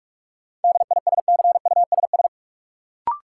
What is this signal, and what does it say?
Morse code, heard as a keyed tone, about 46 wpm, 700 Hz. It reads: DISCUSS
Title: Sentences from Top 1000 Words - New Word Drill - 45wpm